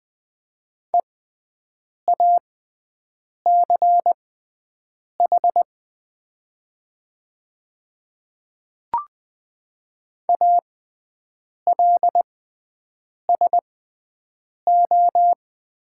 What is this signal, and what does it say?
Morse code, heard as a keyed tone, about 20 wpm, spaced as for 7 wpm, 690 Hz. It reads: EACH ALSO
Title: 6x Character Spacing - Top 200 Words - 20wpm